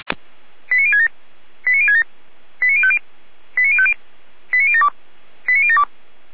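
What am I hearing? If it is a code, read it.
Five-tone selective call (ZVEI), heard as quick runs of five tones, tones of about 70 ms. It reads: 89067 89067 89050 89050 89072 89072
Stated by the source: ZVEI five-tone selective call received with an SDR.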